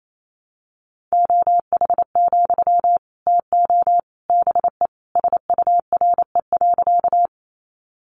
Morse code, heard as a keyed tone, about 28 wpm, 700 Hz. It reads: OH, TO BE SURE.
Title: Sentences from Top 500 Words 28wpm